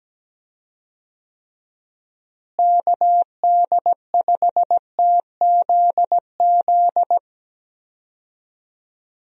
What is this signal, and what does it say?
Morse code, heard as a keyed tone, about 17 wpm, 700 Hz. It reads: KD5TZZ
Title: USA Call Signs - 6 Characters - Repeated 2x - 17wpm